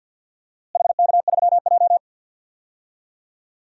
Morse code, heard as a keyed tone, about 48 wpm, 700 Hz. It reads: HK3J
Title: DXCC-100 Call Signs - 4 Characters - Repeated 2x - 50wpm